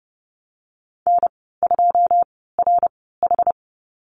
Morse code, heard as a keyed tone, about 30 wpm, spaced as for 17 wpm, 700 Hz. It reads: N2RH
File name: Lesson 14g - Number 2 - Call Signs 30wpm